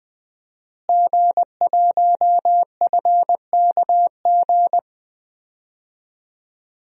Morse code, heard as a keyed tone, about 20 wpm, 700 Hz. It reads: G1FKG
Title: DXCC-100 Call Signs - 5 Characters - Repeated 2x - 20wpm